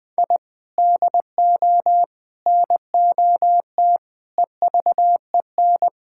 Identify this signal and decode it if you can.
Morse code, heard as a keyed tone, about 20 wpm, 700 Hz. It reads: I DO NOT EVEN
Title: Sentences with 5-Word Phrases - 20wpm